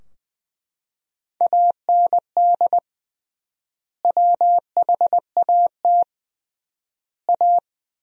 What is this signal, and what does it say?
Morse code, heard as a keyed tone, about 21 wpm, 700 Hz. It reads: AND WHAT A
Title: 3x Word-Spacing - Sentences from Top 500 Words - 20wpm